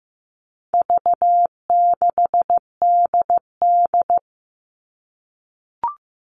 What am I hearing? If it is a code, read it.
Morse code, heard as a keyed tone, about 15 wpm, 700 Hz. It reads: V6DD
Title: All Cut Numbers - 4-Digit 15wpm